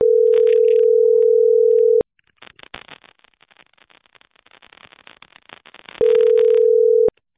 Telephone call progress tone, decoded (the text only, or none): ringback tone